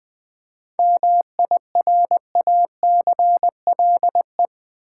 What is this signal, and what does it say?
Morse code, heard as a keyed tone, about 20 wpm, 700 Hz. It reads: MIRACLE
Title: 7 Letter Words - First 3 Letters Given - 20wpm